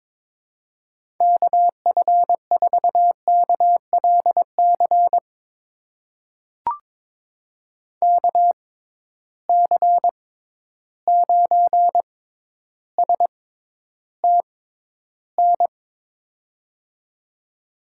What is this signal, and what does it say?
Morse code, heard as a keyed tone, about 22 wpm, spaced as for 11 wpm, 700 Hz. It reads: KF4KLC KC9STN
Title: USA Call Signs - 6 Characters - 6x Character Spacing - 22wpm